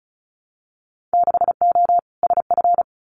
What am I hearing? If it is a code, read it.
Morse code, heard as a keyed tone, about 36 wpm, 700 Hz. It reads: 6O SF
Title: Winter Field Day Exchanges Encore 35wpm